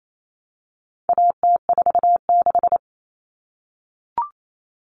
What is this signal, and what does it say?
Morse code, heard as a keyed tone, about 28 wpm, 700 Hz. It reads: AT46
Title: All Cut Numbers - 4-Digit 28wpm